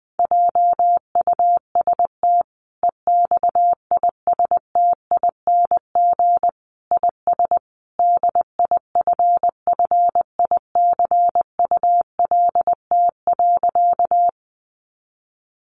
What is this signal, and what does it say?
Morse code, heard as a keyed tone, about 20 wpm, 700 Hz. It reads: JUST EXISTING IS DIFFICULT.
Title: Sentences from Top 1500 Words 20wpm